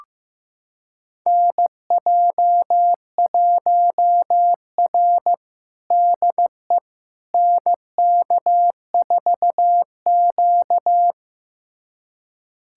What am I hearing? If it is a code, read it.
Morse code, heard as a keyed tone, about 16 wpm, 700 Hz. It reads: NJ1R DE NK4Q
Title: USA Callsign DE Callsign - 4 Characters - 15wpm